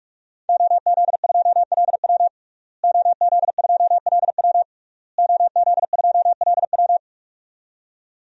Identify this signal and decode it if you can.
Morse code, heard as a keyed tone, about 46 wpm, 700 Hz. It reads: OZ2LW OZ2LW OZ2LW
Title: DXCC-100 Call Signs - 5 Characters - Repeated 3x - 45wpm